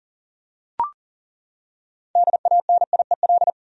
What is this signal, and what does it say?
Morse code, heard as a keyed tone, about 40 wpm, 700 Hz. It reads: DANIEL
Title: USA Common Names - 40wpm